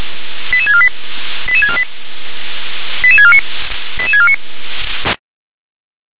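Five-tone selective call (ZVEI), heard as quick runs of five tones, tones of about 70 ms. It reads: 88648 88648 88649 88649
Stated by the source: Received on SDR, ZVEI five-tone call 88747.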